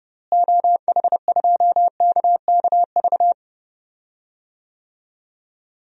Morse code, heard as a keyed tone, about 30 wpm, 700 Hz. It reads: OH2KKV
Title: DXCC-100 Call Signs - 6 Characters - Repeated 3x - 30wpm